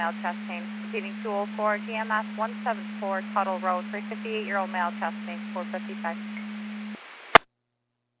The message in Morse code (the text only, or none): none